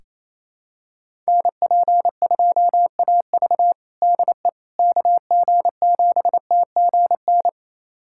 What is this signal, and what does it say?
Morse code, heard as a keyed tone, about 28 wpm, 700 Hz. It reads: NP2AV DE KG7TGN
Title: USA Callsign DE Callsign 28wpm